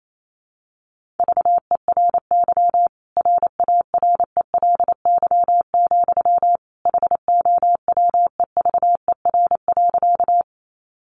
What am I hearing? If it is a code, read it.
Morse code, heard as a keyed tone, about 28 wpm, 700 Hz. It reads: VERY RARELY, HOWEVER.
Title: Sentences from Top 1400 Words 28wpm